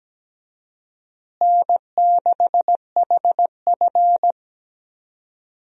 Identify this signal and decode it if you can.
Morse code, heard as a keyed tone, about 17 wpm, 700 Hz. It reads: N6HF
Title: USA Call Signs - 4 Characters - Repeated 2x - 17wpm